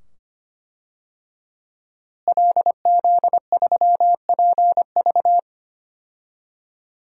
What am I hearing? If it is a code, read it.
Morse code, heard as a keyed tone, about 25 wpm, 700 Hz. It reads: LZ3PV
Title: DXCC-100 Call Signs - 5 Characters - Repeated 2x - 25wpm